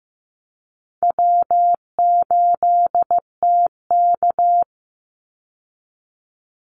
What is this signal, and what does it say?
Morse code, heard as a keyed tone, about 15 wpm, 700 Hz. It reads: W8TK
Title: Easy Call Signs Repeated 3x - 15wpm